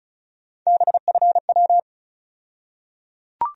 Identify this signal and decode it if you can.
Morse code, heard as a keyed tone, about 36 wpm, 700 Hz. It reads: BFW
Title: USA Call Sign Postfixes - 3 Letters 35wpm